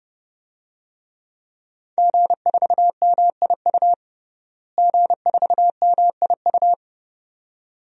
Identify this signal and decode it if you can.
Morse code, heard as a keyed tone, about 30 wpm, 700 Hz. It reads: G4MIU G4MIU
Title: DXCC-100 Call Signs - 5 Characters - Repeated 2x - 30wpm